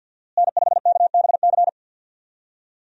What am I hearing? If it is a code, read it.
Morse code, heard as a keyed tone, about 50 wpm, 700 Hz. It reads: N5KBC
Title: USA Call Signs - 5 Characters - Repeated 3x - 50wpm